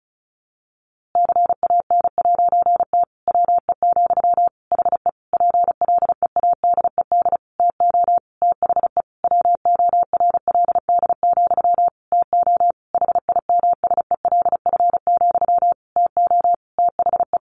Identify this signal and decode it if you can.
Morse code, heard as a keyed tone, about 34 wpm, 700 Hz. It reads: CAN'T WE, HE PLEADED TO THE WORLD, TO HIMSELF, TO THE
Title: A Sound Of Thunder 35wpm